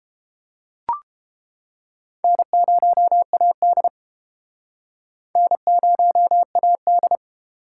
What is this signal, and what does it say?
Morse code, heard as a keyed tone, about 31 wpm, 700 Hz. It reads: N0AD N0AD